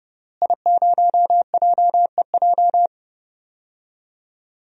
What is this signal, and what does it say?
Morse code, heard as a keyed tone, about 30 wpm, 700 Hz. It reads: I0JEJ
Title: DXCC-100 Call Signs - 5 Characters - Repeated 2x - 30wpm